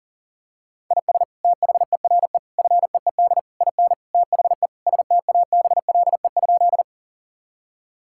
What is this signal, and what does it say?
Morse code, heard as a keyed tone, about 40 wpm, 700 Hz. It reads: IS THERE FEED IN THE STABLE?
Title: Sentences from Top 1600 Words 40wpm